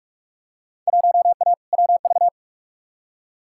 Morse code, heard as a keyed tone, about 43 wpm, 700 Hz. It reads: 1A WV
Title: Field Day Exchanges Encore - 45wpm